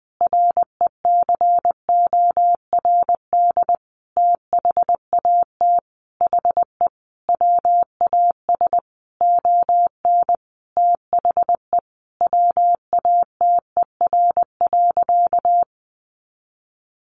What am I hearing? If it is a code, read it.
Morse code, heard as a keyed tone, about 20 wpm, 700 Hz. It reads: RECORD THAT HE WAS ON THE WATER.
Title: Sentences from Top 500 Words 20wpm